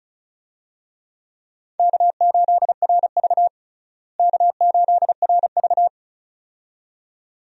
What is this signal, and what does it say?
Morse code, heard as a keyed tone, about 36 wpm, 700 Hz. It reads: K8RV K8RV
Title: USA Call Signs - 4 Characters - Repeated 2x - 35wpm